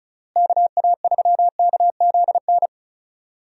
Morse code, heard as a keyed tone, about 34 wpm, 700 Hz. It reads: KA3KZN
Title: USA Call Signs - 6 Characters - Repeated 3x - 35wpm